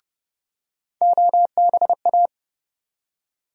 Morse code, heard as a keyed tone, about 31 wpm, 700 Hz. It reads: OBA
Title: USA Call Sign Postfixes - 3 Letters 30wpm